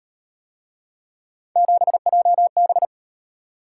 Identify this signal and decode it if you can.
Morse code, heard as a keyed tone, about 34 wpm, 700 Hz. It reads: 7JB